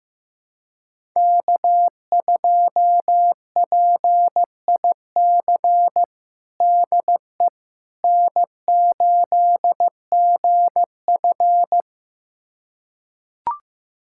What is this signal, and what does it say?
Morse code, heard as a keyed tone, about 15 wpm, 700 Hz. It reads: K2PIC DE N8GF
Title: USA Callsign DE Callsign 15wpm